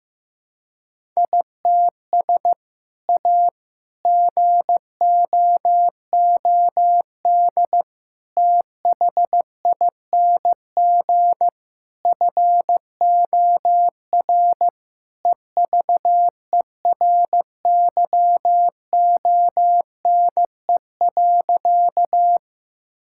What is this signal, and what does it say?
Morse code, heard as a keyed tone, about 15 wpm, 700 Hz. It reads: ITS A GOOD THING FOR EVERYONE.
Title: Sentences from Top 1000 Words - Review Part 5 - 15wpm